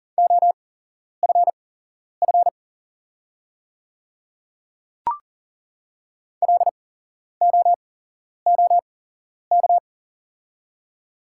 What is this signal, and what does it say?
Morse code, heard as a keyed tone, about 40 wpm, spaced as for 11 wpm, 690 Hz. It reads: OFF LOOK
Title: 8x Character Spacing - Top 200 Words - 40wpm